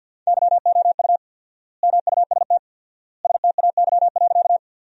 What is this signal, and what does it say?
Morse code, heard as a keyed tone, about 46 wpm, 700 Hz. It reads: YOU MUST STAY.
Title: 4x Word-Spacing - Sentences from Top 500 Words - 50wpm